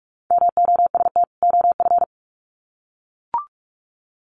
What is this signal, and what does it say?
Morse code, heard as a keyed tone, about 44 wpm, 700 Hz. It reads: MOST OF
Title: Sets of 2 Words 45wpm